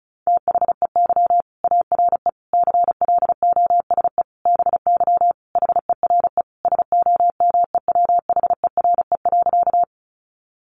Morse code, heard as a keyed tone, about 34 wpm, 700 Hz. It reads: THEY ARE CLOSE BY HERE SOMEWHERE.
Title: Sentences from Top 1400 Words 35wpm